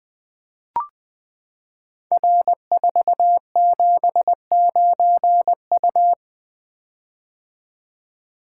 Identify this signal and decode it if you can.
Morse code, heard as a keyed tone, about 20 wpm, 700 Hz. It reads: R479U
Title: Code Groups - 5 Letter:Numbers - 20wpm